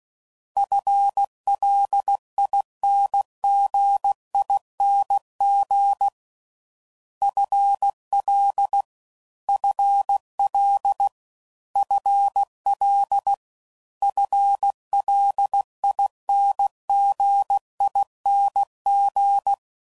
Morse code, heard as a keyed tone, about 16 wpm, 800 Hz. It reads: FLINGING FL FL FL FLINGING